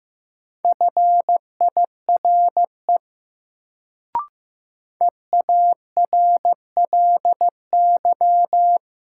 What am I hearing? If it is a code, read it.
Morse code, heard as a keyed tone, about 15 wpm, 700 Hz. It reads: FIRE EARLY
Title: Sentences from Top 400 Words - New Word Drill - 15wpm